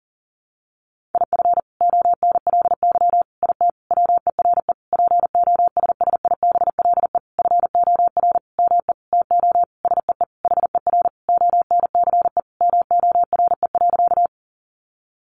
Morse code, heard as a keyed tone, about 40 wpm, 700 Hz. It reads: IF ONLY IT WERE POSSIBLE FOR ME TO SEE HER ONCE MORE.
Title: Sentences from Top 500 Words 40wpm